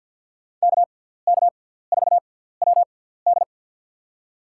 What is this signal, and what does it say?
Morse code, heard as a keyed tone, about 53 wpm, spaced as for 17 wpm, 700 Hz. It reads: KK4WD
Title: USA Call Signs - 5 Characters - 6x Character Spacing - 50wpm